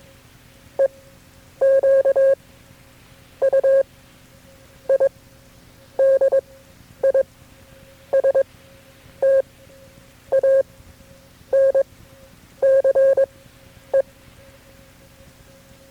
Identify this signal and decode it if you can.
Morse code, heard as a keyed tone, about 22 wpm, spaced as for 8 wpm, 540 Hz. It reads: EQUIDISTANCE
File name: FreeFiles / CW-practice / KB6MT-tape3-side1